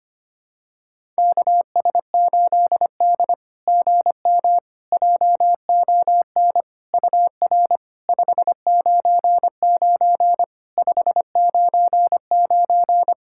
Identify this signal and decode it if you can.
Morse code, heard as a keyed tone, about 25 wpm, 700 Hz. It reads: KS8D GM JON UR 599 599
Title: SOTA QSO Practice 25wpm